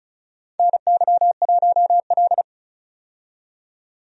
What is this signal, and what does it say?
Morse code, heard as a keyed tone, about 34 wpm, 700 Hz. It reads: NY1L
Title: USA Call Signs - 4 Characters - 35wpm